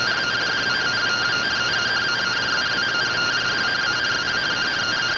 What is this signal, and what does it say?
Two-tone teleprinter signal, about 170 Hz shift.